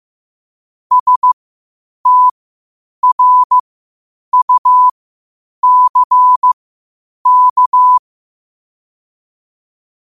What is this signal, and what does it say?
Morse code, heard as a keyed tone, about 15 wpm, spaced as for 8 wpm, 1000 Hz. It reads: STRUCK
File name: qotdfiles / 8wpm